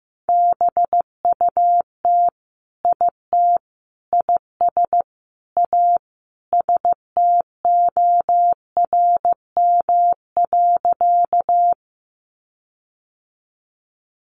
Morse code, heard as a keyed tone, about 15 wpm, 700 Hz. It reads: BUT IT IS A STORM.
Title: Sentences from Top 1500 Words 15wpm